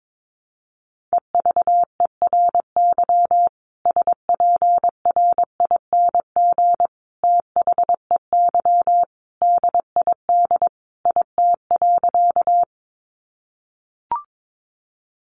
Morse code, heard as a keyed tone, about 22 wpm, 700 Hz. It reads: EVERY SPRING THEY DID IT.
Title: Sentences from Top 1200 Words 22wpm